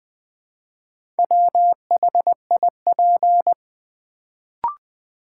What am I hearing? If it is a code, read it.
Morse code, heard as a keyed tone, about 20 wpm, 700 Hz. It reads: WHIP
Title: Build A Word - 1 Letter At A Time - 4-Letter Words - 20wpm